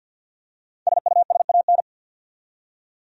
Morse code, heard as a keyed tone, about 46 wpm, 700 Hz. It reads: SUSAN
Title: USA Common Names - 50wpm